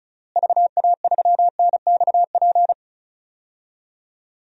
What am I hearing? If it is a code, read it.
Morse code, heard as a keyed tone, about 36 wpm, 700 Hz. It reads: VA3NXP
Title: DXCC-100 Call Signs - 6 Characters - Repeated 2x - 35wpm